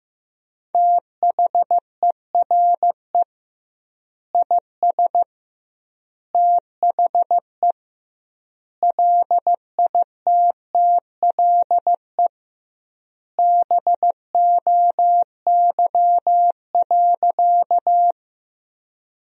Morse code, heard as a keyed tone, about 15 wpm, 700 Hz. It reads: THERE IS THE LITTLE BOY.
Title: 2x Word-Spacing - Sentences from Top 200 Words - 15wpm